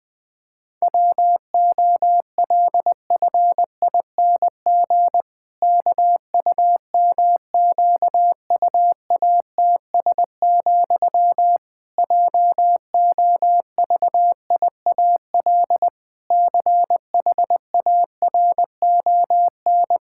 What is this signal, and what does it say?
Morse code, heard as a keyed tone, about 20 wpm, 700 Hz. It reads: WOLFING KUMQUATS, JOVIAL CHARON
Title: Pangram Challenge v2 - 20wpm